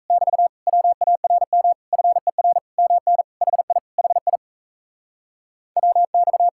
Morse code, heard as a keyed tone, about 37 wpm, 700 Hz. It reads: <BT> WARM FER MN HI HI WX